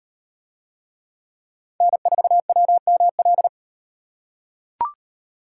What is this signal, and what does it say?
Morse code, heard as a keyed tone, about 38 wpm, 700 Hz. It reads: N4WML